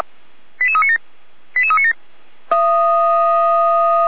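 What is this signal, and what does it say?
Five-tone selective call (ZVEI), heard as quick runs of five tones, tones of about 70 ms. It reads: 88397 88397
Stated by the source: Five-tone ZVEI selective call received by SDR, from the feuerwehr-orlamuende.org fire brigade audio archive.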